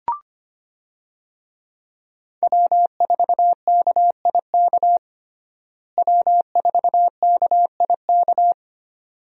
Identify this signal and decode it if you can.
Morse code, heard as a keyed tone, about 25 wpm, 700 Hz. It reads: W4KIK W4KIK
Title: USA Call Signs - 5 Characters - Repeated 2x - 25wpm